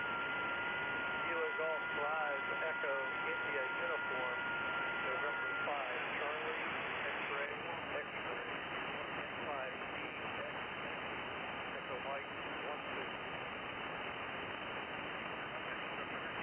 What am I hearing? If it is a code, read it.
FT8, with speech.